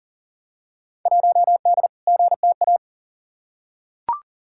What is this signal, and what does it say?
Morse code, heard as a keyed tone, about 37 wpm, 700 Hz. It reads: 1D GTA
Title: Field Day Exchanges - 40wpm